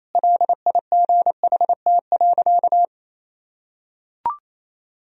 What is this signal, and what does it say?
Morse code, heard as a keyed tone, about 28 wpm, 700 Hz. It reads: LIGHT.
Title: Sentences from Top 1000 Words - Review Part 1 - 28wpm